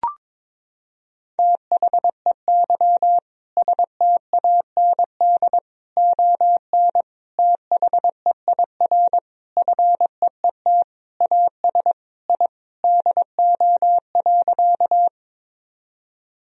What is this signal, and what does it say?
Morse code, heard as a keyed tone, about 22 wpm, 700 Hz. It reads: THEY STAND ON THEIR FEET AS I DO.
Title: Sentences from Top 1000 Words - Review Part 1 - 22wpm